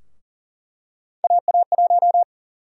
Morse code, heard as a keyed tone, about 40 wpm, 700 Hz. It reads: AA1
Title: USA Call Sign Prefixes - 40wpm